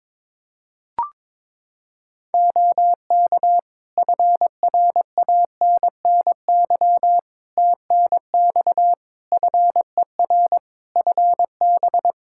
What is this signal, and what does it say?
Morse code, heard as a keyed tone, about 22 wpm, 700 Hz. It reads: OK FRANNY TNX FER FB
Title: Practice QSO - Exchange 4 - Ending - 22wpm